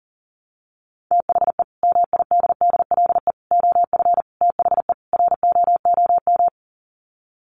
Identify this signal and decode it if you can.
Morse code, heard as a keyed tone, about 37 wpm, 700 Hz. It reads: THE MIDDLE OF THE ROOM
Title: Sets of 5 Words Encore2 40wpm